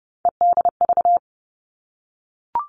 Morse code, heard as a keyed tone, about 30 wpm, 700 Hz. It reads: EDV
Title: All Cut Numbers - 3-Digit 30wpm